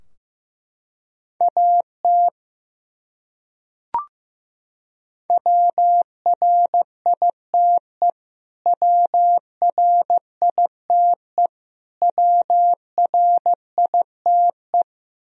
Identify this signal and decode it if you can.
Morse code, heard as a keyed tone, about 15 wpm, 700 Hz. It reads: AT WRITE WRITE WRITE
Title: Top 100 Words Repeated 15wpm